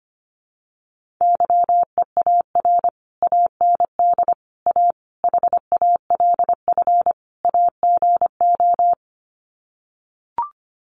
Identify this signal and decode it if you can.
Morse code, heard as a keyed tone, about 25 wpm, 700 Hz. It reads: YEAR AND A HALF AGO
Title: Sets of 5 Words Encore 25wpm